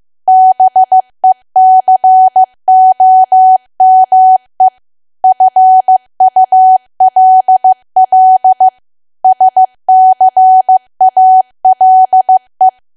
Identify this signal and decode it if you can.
Morse code, heard as a keyed tone, about 15 wpm, 750 Hz. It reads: BECOME FULL SCALE
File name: cwa / W1AW-Level2 / 15WPM